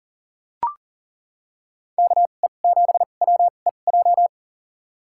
Morse code, heard as a keyed tone, about 40 wpm, spaced as for 29 wpm, 700 Hz. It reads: KE7WEJ
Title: USA Call Signs - 6 Characters - 2x Character Spacing - 40wpm